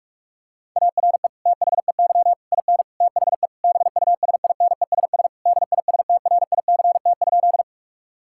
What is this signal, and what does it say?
Morse code, heard as a keyed tone, about 41 wpm, 700 Hz. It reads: ARE THEY IN THE BUSINESS DISTRICT?
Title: Sentences from Top 1600 Words 45wpm